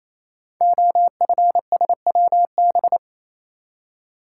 Morse code, heard as a keyed tone, about 28 wpm, 700 Hz. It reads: OFSWB
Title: Code Groups - 5 Letters - 28wpm